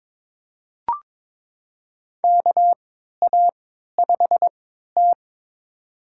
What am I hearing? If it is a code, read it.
Morse code, heard as a keyed tone, about 21 wpm, spaced as for 12 wpm, 700 Hz. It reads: KA5T
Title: USA Call Signs - 4 Characters - 3x Character Spacing - 22wpm